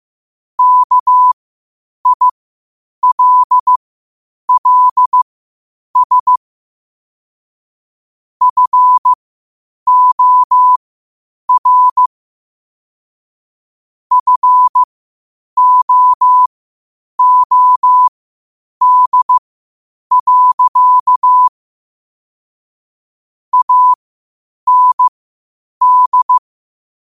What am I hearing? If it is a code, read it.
Morse code, heard as a keyed tone, about 15 wpm, spaced as for 8 wpm, 1000 Hz. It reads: KILLS FOR FOOD. AND
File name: qotdfiles / 8wpm